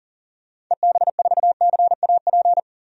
Morse code, heard as a keyed tone, about 37 wpm, 700 Hz. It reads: EB4CAP
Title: DXCC-100 Call Signs - 6 Characters - 40wpm